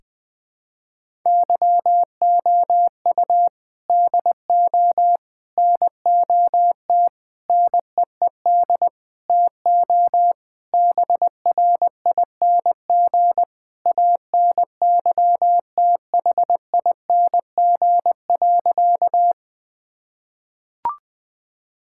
Morse code, heard as a keyed tone, about 20 wpm, 700 Hz. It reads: YOU DO NOT NEED TO BRING ANYTHING.
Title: Sentences with 4-Word Phrases - Spoken Beginning - 20wpm